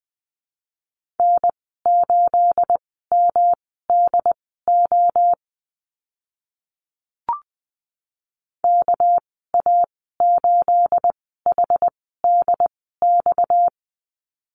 Morse code, heard as a keyed tone, about 20 wpm, spaced as for 15 wpm, 700 Hz. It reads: N8MDO KA8HDX
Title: 238 Lesson 38g - Letter q - Call Signs 20wpm